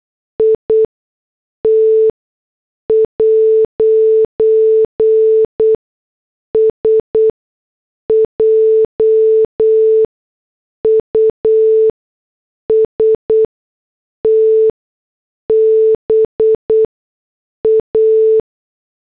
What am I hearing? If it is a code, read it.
Morse code, heard as a keyed tone, about 8 wpm, spaced as for 6 wpm, 440 Hz. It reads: IT'S JUST BA
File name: morse / headlines